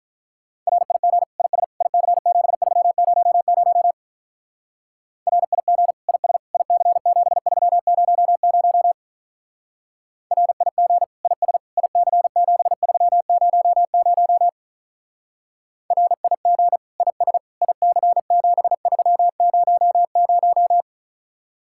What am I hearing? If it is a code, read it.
Morse code, heard as a keyed tone, about 44 wpm, 700 Hz. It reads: RIG IS IC7300 RIG IS IC7300 RIG IS IC7300 RIG IS IC7300